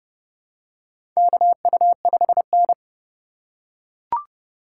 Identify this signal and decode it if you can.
Morse code, heard as a keyed tone, about 30 wpm, 700 Hz. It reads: KU5N
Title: USA Call Signs - 4 Characters - 3x Character Spacing - 30wpm